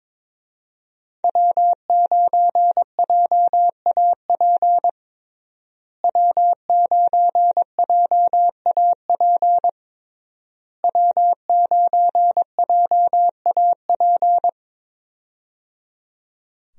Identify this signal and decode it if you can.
Morse code, heard as a keyed tone, about 22 wpm, 700 Hz. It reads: W9JAP W9JAP W9JAP
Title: USA Call Signs - 5 Characters - Repeated 3x - 22wpm